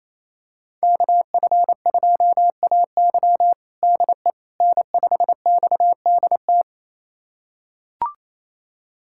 Morse code, heard as a keyed tone, about 28 wpm, 700 Hz. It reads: KF2AY DE N5XDT
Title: USA Callsign DE Callsign - 5 Characters - 28wpm